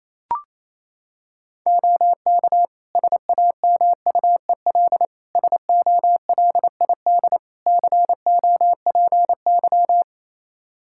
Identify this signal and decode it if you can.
Morse code, heard as a keyed tone, about 28 wpm, 700 Hz. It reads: OK SAMUEL SOLID COPY
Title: K1USN SST QSO Practice 28wpm